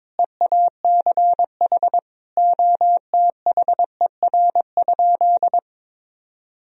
Morse code, heard as a keyed tone, about 22 wpm, 700 Hz. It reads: EACH OTHER?
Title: Sentences from Top 1000 Words - Review Part 1 - 22wpm